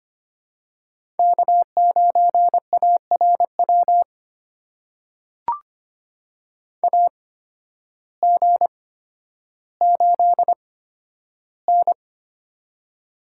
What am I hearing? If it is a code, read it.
Morse code, heard as a keyed tone, about 25 wpm, spaced as for 12 wpm, 700 Hz. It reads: K9ARW AG8N
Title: USA Call Signs - 8x Character Spacing - 25wpm